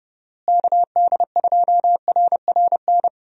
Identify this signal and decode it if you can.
Morse code, heard as a keyed tone, about 30 wpm, 700 Hz. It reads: KD2RRN
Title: USA Call Signs - 6 Characters - Repeated 3x - 30wpm